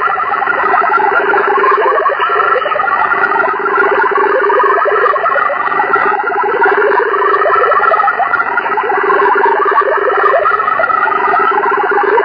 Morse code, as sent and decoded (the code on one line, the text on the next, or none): none